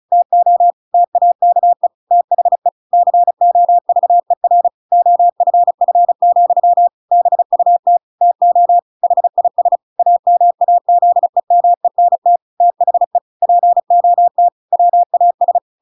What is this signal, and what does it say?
Morse code, heard as a keyed tone, about 36 wpm, 700 Hz. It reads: TO TAKE THE COVER OFF, BUT TO HIS AMAZEMENT THE POT WAS
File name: Pinocchio / Pinocchio ch5 35 35 ik1hge